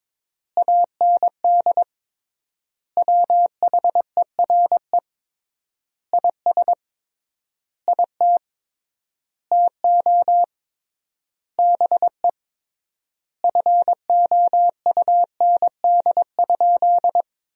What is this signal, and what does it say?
Morse code, heard as a keyed tone, about 22 wpm, 700 Hz. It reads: AND WHERE IS IT TO BE FOUND?
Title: 3x Word-Spacing - Sentences from Top 300 Words - 22wpm